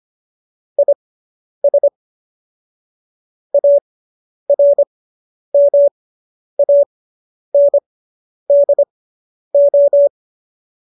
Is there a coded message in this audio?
Yes: Morse code from 0.8 to 10.1 s, heard as a keyed tone, about 25 wpm, spaced as for 10 wpm, 570 Hz. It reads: IS ARMANDO